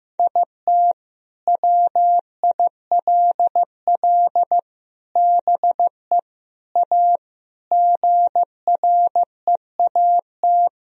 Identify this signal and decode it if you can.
Morse code, heard as a keyed tone, about 15 wpm, 700 Hz. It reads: IT WILL BE A GREAT
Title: Sentences from Top 1000 Words - Review Part 1 - 15wpm